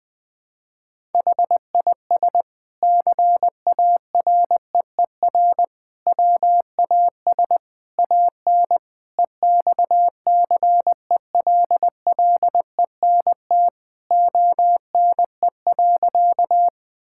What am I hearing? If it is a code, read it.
Morse code, heard as a keyed tone, about 20 wpm, 700 Hz. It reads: HIS CAREER WAS AN EXCELLENT ONE.
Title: Sentences from Top 1000 Words - Review Part 6 - 20wpm